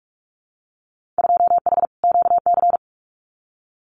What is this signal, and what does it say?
Morse code, heard as a keyed tone, about 46 wpm, 700 Hz. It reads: 2H QC
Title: Winter Field Day Exchanges Encore 45wpm